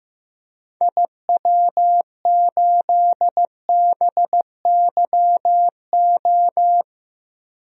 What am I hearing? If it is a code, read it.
Morse code, heard as a keyed tone, about 15 wpm, 700 Hz. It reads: IW8BYO
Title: DXCC-100 Call Signs - 6 Characters - 15wpm